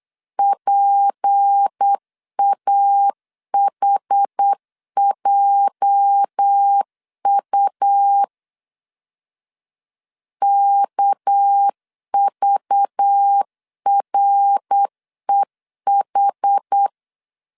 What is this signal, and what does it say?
Morse code, heard as a keyed tone, about 8 wpm, 800 Hz. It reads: PAHJU KVREH
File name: LEZIONI / 8 YJ45 / 8 YJ 8wpm 3313